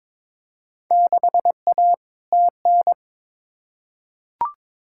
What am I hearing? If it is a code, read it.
Morse code, heard as a keyed tone, about 22 wpm, 700 Hz. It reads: 6A TN
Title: Field Day Exchanges Encore - 22wpm